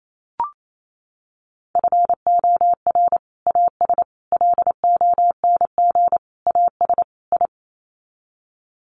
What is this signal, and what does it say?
Morse code, heard as a keyed tone, about 28 wpm, 700 Hz. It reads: FOR AS LONG AS I
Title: Sets of 5 Words Encore2 28wpm